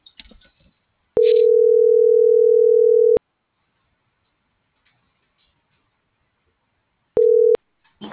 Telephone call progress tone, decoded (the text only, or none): ringback tone